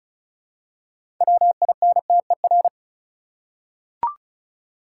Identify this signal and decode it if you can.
Morse code, heard as a keyed tone, about 36 wpm, 700 Hz. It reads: WINTER
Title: Six Letter Words - 35wpm